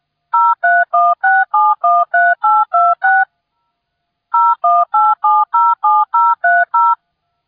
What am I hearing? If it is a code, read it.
DTMF, heard as telephone touch tones, tones of about 220 ms. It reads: 0316713826 018707030